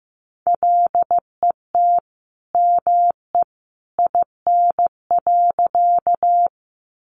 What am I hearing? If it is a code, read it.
Morse code, heard as a keyed tone, about 15 wpm, 700 Hz. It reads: LET ME IN.
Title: Sentences from Top 300 Words 15wpm